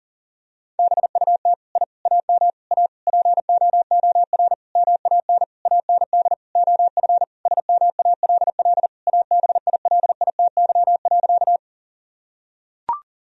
Morse code, heard as a keyed tone, about 40 wpm, 700 Hz. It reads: BUT I AM A POOR MAN AND OF SMALL ABILITY.
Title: Sentences from Top 1000 Words - Review Part 3 - 40wpm